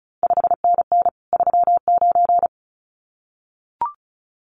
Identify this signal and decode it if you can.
Morse code, heard as a keyed tone, about 36 wpm, 700 Hz. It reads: 5NN 39
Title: CQ WW DX Contest Exchange Encore - 35wpm